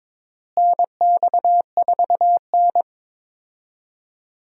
Morse code, heard as a keyed tone, about 22 wpm, 700 Hz. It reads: NX4N
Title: USA Call Signs - 4 Characters - Repeated 3x - 22wpm